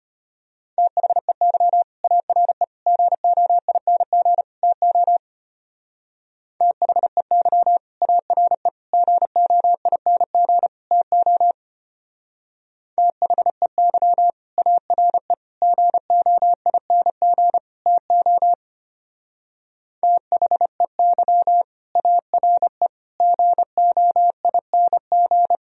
Morse code, heard as a keyed tone, about 32 wpm, 700 Hz. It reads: THEY ARE GOING TO THEY ARE GOING TO THEY ARE GOING TO THEY ARE GOING